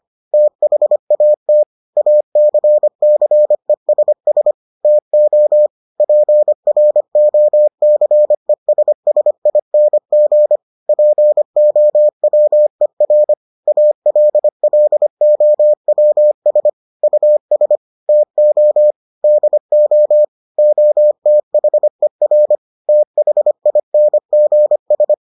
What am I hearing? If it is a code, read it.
Morse code, heard as a keyed tone, about 25 wpm, 600 Hz. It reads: THAT ACCESS TO PROCESSING POWER ALLOWS US TO DO OTHER THINGS